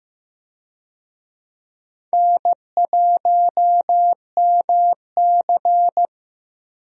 Morse code, heard as a keyed tone, about 15 wpm, 700 Hz. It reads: N1MC
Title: USA Call Signs - 4 Characters - 15wpm